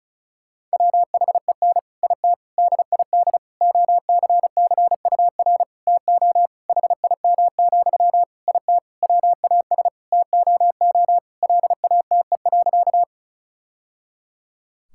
Morse code, heard as a keyed tone, about 36 wpm, 700 Hz. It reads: WHEN IT DID OCCUR TO HIM, IT WAS TOO LATE.
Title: Sentences from Top 1000 Words - Review Part 6 - 35wpm